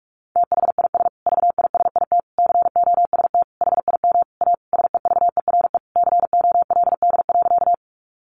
Morse code, heard as a keyed tone, about 44 wpm, 700 Hz. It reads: THIS VISIT COST HIM A SEVERE COLD.
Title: Sentences from Top 1400 Words 45wpm